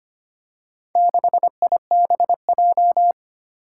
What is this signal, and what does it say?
Morse code, heard as a keyed tone, about 25 wpm, 700 Hz. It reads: 6IBJ
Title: Four Challenging Letter-Number Combinations 25wpm